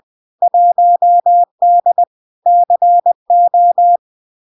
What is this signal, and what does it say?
Morse code, heard as a keyed tone, about 20 wpm, 700 Hz. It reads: 1D CO
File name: LARK-FD-2022-Log-in-CW-20wpm-700-Hz